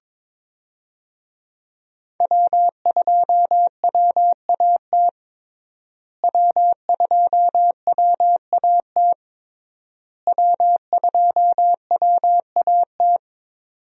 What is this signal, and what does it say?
Morse code, heard as a keyed tone, about 22 wpm, 700 Hz. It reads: W2WAT W2WAT W2WAT
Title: USA Call Signs - 5 Characters - Repeated 3x - 22wpm